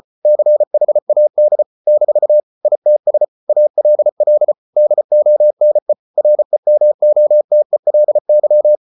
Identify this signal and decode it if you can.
Morse code, heard as a keyed tone, about 34 wpm, 600 Hz. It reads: CHAD - ITS ALL DONE REMOTELY